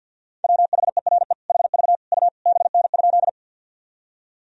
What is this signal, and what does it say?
Morse code, heard as a keyed tone, about 52 wpm, 700 Hz. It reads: WHERE HV U BN?
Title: Sentences from Top 200 Words and Abbreviations 50wpm